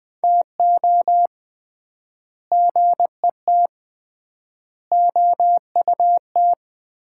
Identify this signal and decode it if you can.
Morse code, heard as a keyed tone, about 20 wpm, 700 Hz. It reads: TO GET OUT
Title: 3x Word-Spacing - Sets of 4 Words - 20wpm